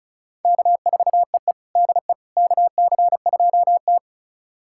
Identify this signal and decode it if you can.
Morse code, heard as a keyed tone, about 34 wpm, 700 Hz. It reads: K4EE DE KC2T
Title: USA Callsign DE Callsign - 4 Characters - 35wpm